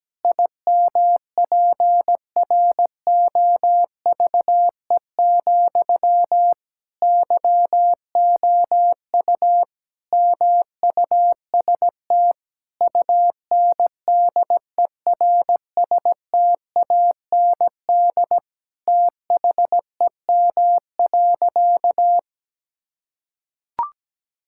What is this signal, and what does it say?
Morse code, heard as a keyed tone, about 17 wpm, 700 Hz. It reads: IMPROVE, YOU MUST UNDERSTAND THEM.
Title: Sentences from Top 1000 Words - Review Part 5 - 17wpm